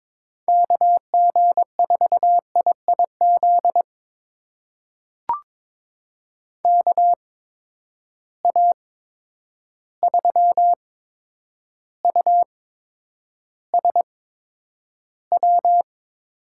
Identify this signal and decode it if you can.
Morse code, heard as a keyed tone, about 22 wpm, spaced as for 9 wpm, 700 Hz. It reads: KG4IIZ KA3USW
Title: USA Call Signs - 6 Characters - 8x Character Spacing - 22wpm